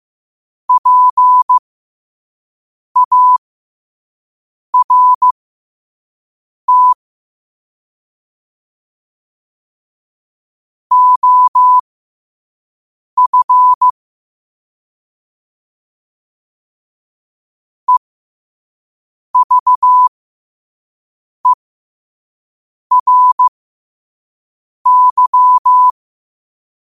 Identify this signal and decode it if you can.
Morse code, heard as a keyed tone, about 15 wpm, spaced as for 5 wpm, 1000 Hz. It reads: PART OF EVERY
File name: qotdfiles / 5wpm